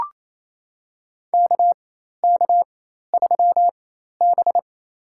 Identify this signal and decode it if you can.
Morse code, heard as a keyed tone, about 28 wpm, spaced as for 13 wpm, 700 Hz. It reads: KK3B
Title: USA Call Signs - 4 Characters - 4x Character Spacing - 28wpm